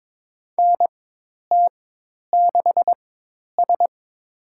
Morse code, heard as a keyed tone, about 22 wpm, spaced as for 10 wpm, 700 Hz. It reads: NT6S
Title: USA Call Signs - 4x Character Spacing - 22wpm